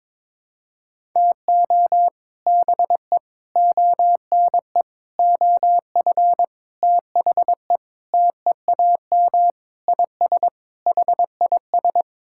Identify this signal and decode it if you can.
Morse code, heard as a keyed tone, about 22 wpm, 700 Hz. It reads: TO BE ONE OF THE TEAM IS HIS
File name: practice / 22wpm / Sentences with 4-Word Phrases - 22wpm